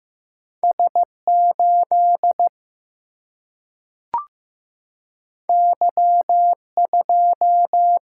Morse code, heard as a keyed tone, about 15 wpm, 700 Hz. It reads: S8 Y2
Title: Single-Letter Single-Number 15wpm